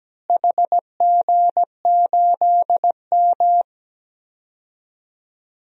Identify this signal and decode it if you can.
Morse code, heard as a keyed tone, about 17 wpm, 700 Hz. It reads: HG8M
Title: DXCC-100 Call Signs - 4 Characters - Repeated 3x - 17wpm